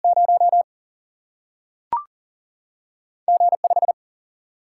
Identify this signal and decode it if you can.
Morse code, heard as a keyed tone, about 39 wpm, 700 Hz. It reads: 0 G5
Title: Single-Letter Single-Number 40wpm